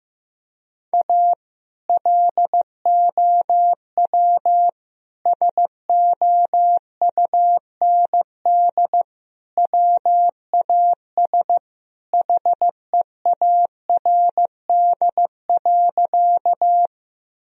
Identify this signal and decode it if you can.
Morse code, heard as a keyed tone, about 15 wpm, 700 Hz. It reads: A LOW SOUND WAS HEARD.
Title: Sentences from Top 1000 Words - Review Part 1 - 15wpm